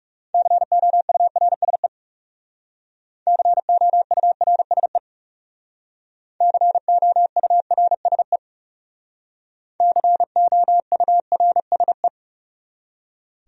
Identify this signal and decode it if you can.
Morse code, heard as a keyed tone, about 35 wpm, 700 Hz. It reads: COURSE COURSE COURSE COURSE
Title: Speed-Racing Top 500 Words 30wpm